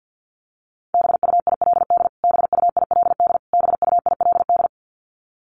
Morse code, heard as a keyed tone, about 50 wpm, 700 Hz. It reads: BUILD BUILD BUILD
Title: Top 300 Words Repeated 50wpm